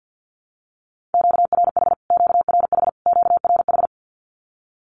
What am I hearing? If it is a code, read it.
Morse code, heard as a keyed tone, about 50 wpm, 700 Hz. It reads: QRH QRH QRH
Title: Q-Codes Repeated 3x 50wpm